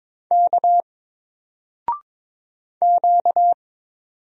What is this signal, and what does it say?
Morse code, heard as a keyed tone, about 22 wpm, 700 Hz. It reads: K Q
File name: Single-Letter-Number-22wpm